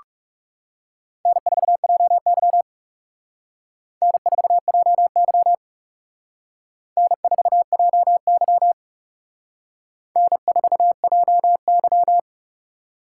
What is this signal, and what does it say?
Morse code, heard as a keyed tone, about 37 wpm, 700 Hz. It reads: N4JY N4JY N4JY N4JY
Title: Speed Racing - USA Call Signs 30wpm